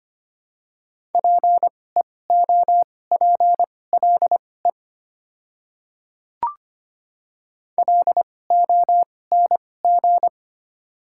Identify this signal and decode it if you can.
Morse code, heard as a keyed tone, about 25 wpm, spaced as for 18 wpm, 700 Hz. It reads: PEOPLE LONG
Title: 2x Character Spacing - Top 100 Words - 25wpm